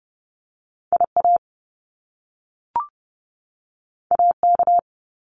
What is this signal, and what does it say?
Morse code, heard as a keyed tone, about 30 wpm, 700 Hz. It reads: IA AK
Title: US States 30wpm